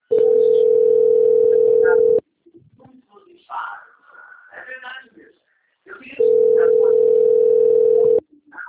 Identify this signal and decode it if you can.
Ringback tone (North America), 440 + 480 Hz, with speech.
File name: out- - - - - .16658